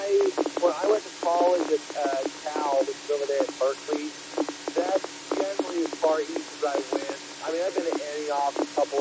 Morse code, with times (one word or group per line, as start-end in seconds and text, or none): none